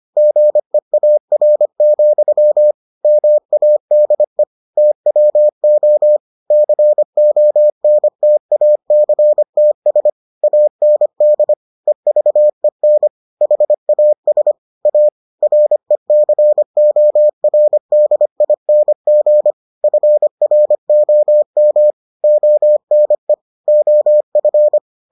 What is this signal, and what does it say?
Morse code, heard as a keyed tone, about 25 wpm, 600 Hz. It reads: GEAR, MADE TWO CONTACTS AND EVEN HAS A RECORDING FROM ONE OF